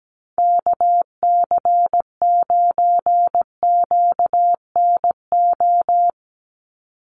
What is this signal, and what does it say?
Morse code, heard as a keyed tone, about 17 wpm, 700 Hz. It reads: KC9QNO
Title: USA Call Signs 17wpm